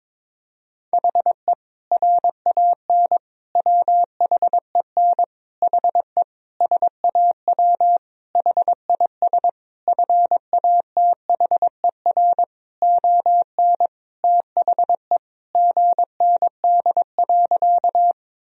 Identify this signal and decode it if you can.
Morse code, heard as a keyed tone, about 22 wpm, 700 Hz. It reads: HE RAN WHEN HE SAW HIS FATHER ON THE GND.
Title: Sentences from Top 500 Words and Abbreviations 22wpm